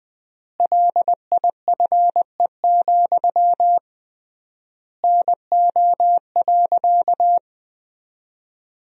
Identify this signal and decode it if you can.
Morse code, heard as a keyed tone, about 20 wpm, 700 Hz. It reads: LIFE, NO.
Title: 3x Word-Spacing - Sentences from Top 300 Words - 20wpm